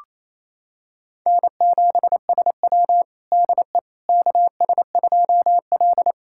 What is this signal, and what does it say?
Morse code, heard as a keyed tone, about 28 wpm, 700 Hz. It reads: N7SW DE KS2L
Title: USA Callsign DE Callsign - 4 Characters - 28wpm